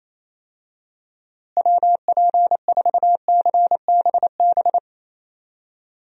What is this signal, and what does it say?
Morse code, heard as a keyed tone, about 28 wpm, 700 Hz. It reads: WP4CBB
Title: USA Call Signs - 6 Characters - Repeated 3x - 28wpm